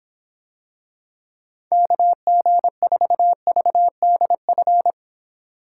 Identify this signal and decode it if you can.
Morse code, heard as a keyed tone, about 26 wpm, 700 Hz. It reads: KG4VDF